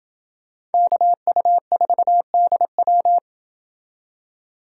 Morse code, heard as a keyed tone, about 28 wpm, 700 Hz. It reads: KU4DW